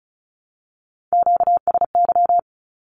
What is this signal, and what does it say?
Morse code, heard as a keyed tone, about 34 wpm, 700 Hz. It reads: QSY
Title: Q-Codes Repeated 3x 35wpm